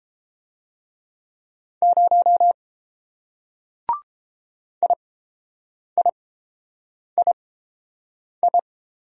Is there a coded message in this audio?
Yes: Morse code from 1.8 to 8.6 s, heard as a keyed tone, about 30 wpm, spaced as for 7 wpm, 700 Hz. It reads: 0 IIII